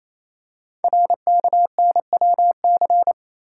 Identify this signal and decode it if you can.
Morse code, heard as a keyed tone, about 27 wpm, 700 Hz. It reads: RKNWC
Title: Code Groups - 5 Letters - 28wpm